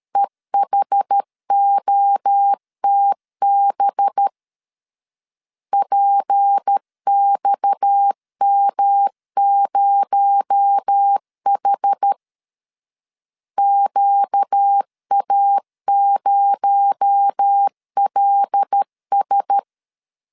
Morse code, heard as a keyed tone, about 13 wpm, 800 Hz. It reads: EHOTB PXM0H QA0LS
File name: LEZIONI / 6 PKX0 / 6 PKX0 12wpm 2231